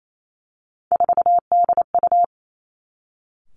Morse code, heard as a keyed tone, about 28 wpm, 700 Hz. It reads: 4DU
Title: Three Challenging Letter-Number Combinations 28wpm